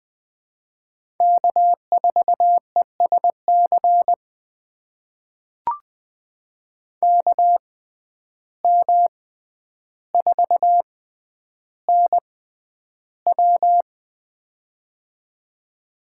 Morse code, heard as a keyed tone, about 20 wpm, spaced as for 10 wpm, 700 Hz. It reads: K4ESC KM4NW
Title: USA Call Signs - 5 Characters - 6x Character Spacing - 20wpm